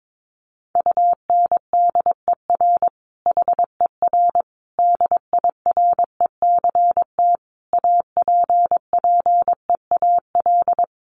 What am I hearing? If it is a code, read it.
Morse code, heard as a keyed tone, about 22 wpm, 700 Hz. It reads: UNDER HER DIRECT APPEAL
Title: Sentences from Top 1300 Words 22wpm